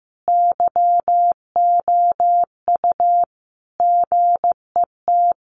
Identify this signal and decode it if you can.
Morse code, heard as a keyed tone, about 15 wpm, 700 Hz. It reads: YOU GET
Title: Sentences from Top 600 Words 15wpm